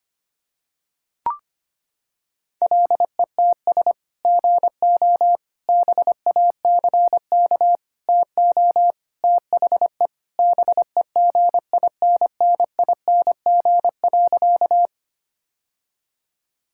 Morse code, heard as a keyed tone, about 25 wpm, 700 Hz. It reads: LETS GO BACK TO THE BEGINNING.
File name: practice / 25wpm / Sentences with 3-Word Phrases - 25wpm